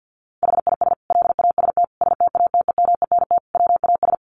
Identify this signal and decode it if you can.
Morse code, heard as a keyed tone, about 50 wpm, 700 Hz. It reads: HIS LAST STATEMENT WAS
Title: Sentences from Top 1200 Words 50wpm